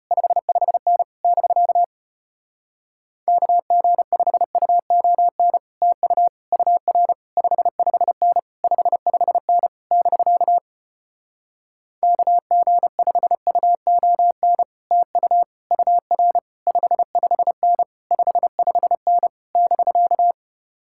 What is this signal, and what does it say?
Morse code, heard as a keyed tone, about 33 wpm, 700 Hz. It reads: 55N <BK> KG5UON TU UR 55N 55N <BK> KG5UON TU UR 55N 55N <BK>